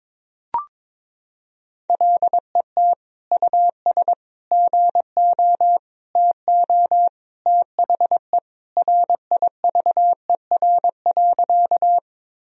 Morse code, heard as a keyed tone, about 22 wpm, 700 Hz. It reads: LET US GO TO THE RIVER.
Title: Sentences from Top 1000 Words - Review Part 1 - 22wpm